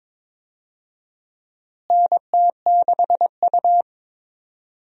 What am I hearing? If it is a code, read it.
Morse code, heard as a keyed tone, about 22 wpm, 700 Hz. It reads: NT6U
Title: USA Call Signs - 4 Characters - Repeated 2x - 22wpm